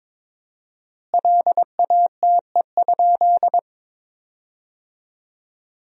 Morse code, heard as a keyed tone, about 23 wpm, 700 Hz. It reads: LATE?
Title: 4x Word-Spacing - Sentences from Top 300 Words - 22wpm